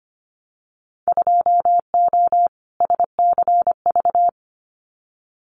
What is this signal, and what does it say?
Morse code, heard as a keyed tone, about 25 wpm, 700 Hz. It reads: 2O SCV
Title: Winter Field Day Exchanges Encore 25wpm